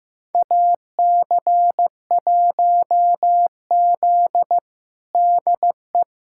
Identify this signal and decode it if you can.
Morse code, heard as a keyed tone, about 16 wpm, 700 Hz. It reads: AC1Z DE
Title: USA Callsign DE Callsign 15wpm